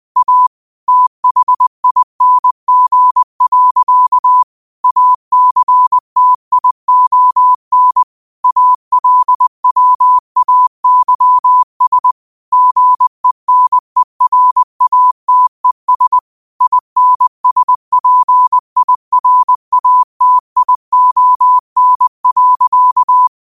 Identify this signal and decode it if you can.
Morse code, heard as a keyed tone, about 20 wpm, 1000 Hz. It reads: A THING. ACTION ALWAYS GENERATES INSPIRATION.
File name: qotdfiles / 20wpm